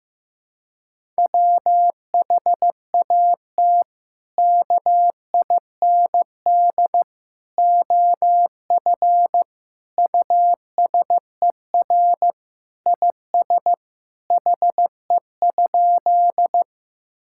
Morse code, heard as a keyed tone, about 15 wpm, 700 Hz. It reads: WHAT KIND OF USER IS HE?
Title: Sentences from Top 1000 Words - Review Part 5 - 15wpm